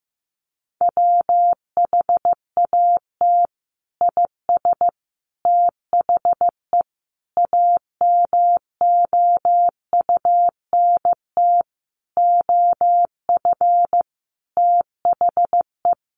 Morse code, heard as a keyed tone, about 15 wpm, 700 Hz. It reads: WHAT IS THE AMOUNT OF THE
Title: Sentences from Top 1500 Words 15wpm